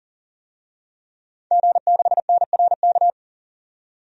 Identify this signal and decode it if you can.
Morse code, heard as a keyed tone, about 40 wpm, 700 Hz. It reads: G6NRK
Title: DXCC-100 Call Signs - 5 Characters - 40wpm